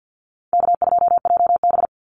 Morse code, heard as a keyed tone, about 50 wpm, 700 Hz. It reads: K2JB
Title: NA SOTA Activator Callsigns 50wpm